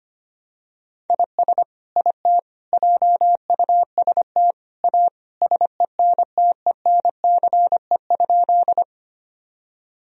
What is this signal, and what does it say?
Morse code, heard as a keyed tone, about 25 wpm, 700 Hz. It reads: IS IT JUST A SENTENCE?
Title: Sentences from Top 1000 Words - Review Part 1 - 25wpm